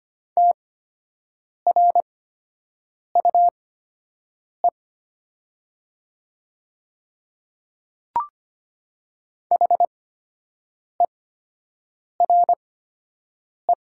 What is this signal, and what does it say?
Morse code, heard as a keyed tone, about 25 wpm, spaced as for 7 wpm, 700 Hz. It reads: TRUE HERE
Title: 8x Character Spacing - Top 500 Words - 25wpm